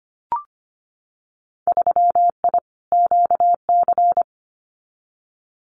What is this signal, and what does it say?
Morse code, heard as a keyed tone, about 25 wpm, 700 Hz. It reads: 3I QC
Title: Winter Field Day Exchanges Encore 25wpm